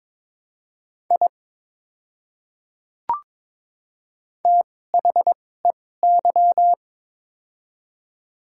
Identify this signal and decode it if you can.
Morse code, heard as a keyed tone, about 22 wpm, spaced as for 16 wpm, 700 Hz. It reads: I THEY
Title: 2x Character Spacing - Top 200 Words - 22wpm